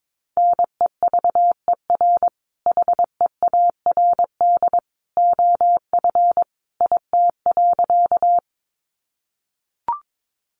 Morse code, heard as a keyed tone, about 22 wpm, 700 Hz. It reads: NEVER HEARD OF IT.
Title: Sentences from Top 400 Words 22wpm